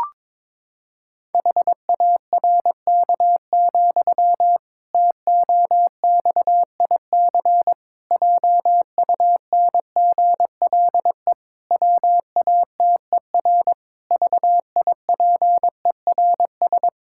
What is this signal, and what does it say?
Morse code, heard as a keyed tone, about 22 wpm, 700 Hz. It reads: HARK, TOXIC JUNGLE WATER VIPERS
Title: Pangram Challenge v2 - 22wpm